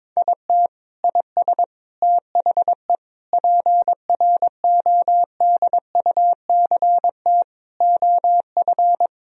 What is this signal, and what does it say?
Morse code, heard as a keyed tone, about 22 wpm, 700 Hz. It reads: IT IS THE PRODUCT OF
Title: Sentences from Top 1000 Words - Review Part 2 - 22wpm